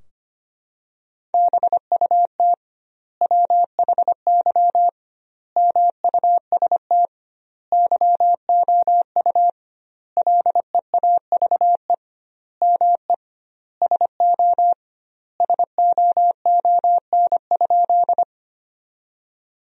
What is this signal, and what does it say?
Morse code, heard as a keyed tone, about 25 wpm, 700 Hz. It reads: BUT WHY MUST YOU LEAVE ME SO SOON?
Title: 2x Word-Spacing - Sentences from Top 400 Words - 25wpm